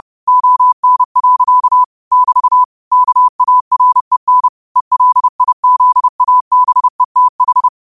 Morse code, heard as a keyed tone, about 30 wpm, 1000 Hz. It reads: ON. <BT> KAREN ELIZABETH
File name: qotdfiles / 30wpm